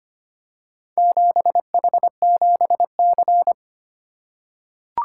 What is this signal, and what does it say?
Morse code, heard as a keyed tone, about 25 wpm, 700 Hz. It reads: 7H7C
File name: practice / 25wpm / Four Letter-Number Combinations - 25wpm